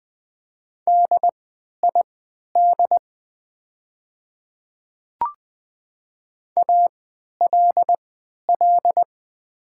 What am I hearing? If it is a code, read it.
Morse code, heard as a keyed tone, about 20 wpm, spaced as for 11 wpm, 700 Hz. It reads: DID ALL
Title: 3x Character Spacing - Top 100 Words - 20wpm